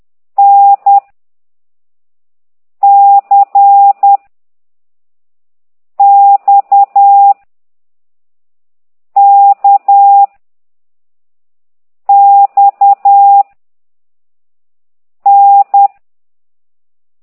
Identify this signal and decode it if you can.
Morse code, heard as a keyed tone, about 10 wpm, 800 Hz. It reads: N C X K X N
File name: ct2idw / exerc4